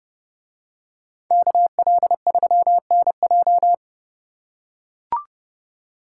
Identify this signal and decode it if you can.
Morse code, heard as a keyed tone, about 28 wpm, 700 Hz. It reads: KL3NJ